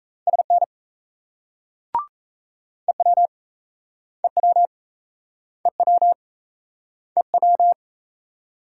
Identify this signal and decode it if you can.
Morse code, heard as a keyed tone, about 36 wpm, 700 Hz. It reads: SN EW EW EW EW